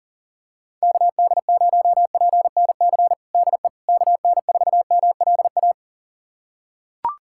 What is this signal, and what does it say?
Morse code, heard as a keyed tone, about 40 wpm, 700 Hz. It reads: KD0PNC DE KN4MLA
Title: USA Callsign DE Callsign - 6 Characters - 40wpm